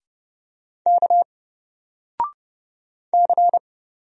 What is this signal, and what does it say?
Morse code, heard as a keyed tone, about 30 wpm, 700 Hz. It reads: K C
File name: Single-Letter-Number-30wpm